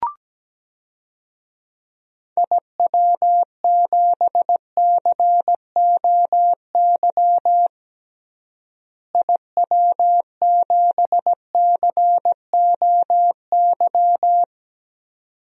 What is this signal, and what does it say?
Morse code, heard as a keyed tone, about 17 wpm, 700 Hz. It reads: IW7COY IW7COY
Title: DXCC-100 Call Signs - 6 Characters - Repeated 2x - 17wpm